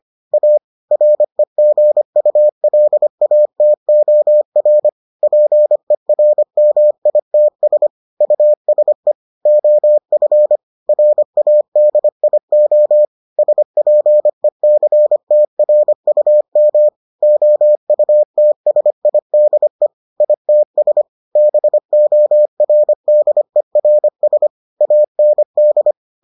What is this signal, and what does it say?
Morse code, heard as a keyed tone, about 25 wpm, 600 Hz. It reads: A REGULATOR PERMITS USE OF RADIO SPECTRUM OUTSIDE ITS BORDERS AND